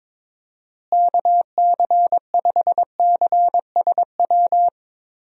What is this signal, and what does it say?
Morse code, heard as a keyed tone, about 22 wpm, 700 Hz. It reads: KC5CSW
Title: USA Call Signs - 6 Characters - Repeated 3x - 22wpm